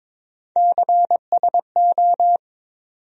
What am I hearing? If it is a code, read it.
Morse code, heard as a keyed tone, about 22 wpm, 700 Hz. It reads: CSO
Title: USA Call Sign Postfixes - 3 Letters 22wpm